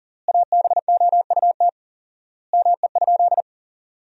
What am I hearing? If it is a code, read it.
Morse code, heard as a keyed tone, about 37 wpm, 700 Hz. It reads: ABOUT ME?
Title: 4x Word-Spacing - Sentences from Top 200 Words - 40wpm